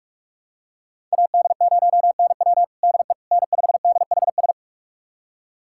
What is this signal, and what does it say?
Morse code, heard as a keyed tone, about 46 wpm, 700 Hz. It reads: AD0NW DE N5DHS
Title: USA Callsign DE Callsign - 5 Characters - 45wpm